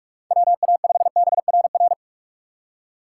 Morse code, heard as a keyed tone, about 46 wpm, 700 Hz. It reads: WA5BRR
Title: USA Call Signs - 6x Character Spacing - 45wpm